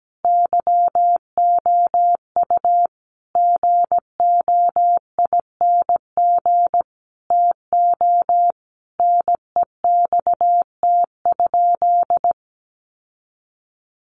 Morse code, heard as a keyed tone, about 17 wpm, 700 Hz. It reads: YOU GOING TO NEXT?
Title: Sentences from Top 1200 Words 17wpm